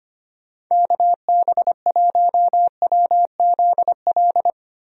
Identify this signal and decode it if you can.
Morse code, heard as a keyed tone, about 25 wpm, 700 Hz. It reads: KB1WZL
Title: USA Call Signs - 6 Characters - Repeated 2x - 25wpm